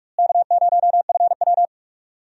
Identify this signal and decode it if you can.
Morse code, heard as a keyed tone, about 46 wpm, 700 Hz. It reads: K0FW
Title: USA Call Signs - 4 Characters - Repeated 3x - 45wpm